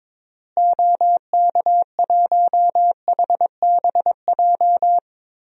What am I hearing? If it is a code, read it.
Morse code, heard as a keyed tone, about 22 wpm, 700 Hz. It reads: OK1HBJ
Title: DXCC-100 Call Signs - 6 Characters - Repeated 3x - 22wpm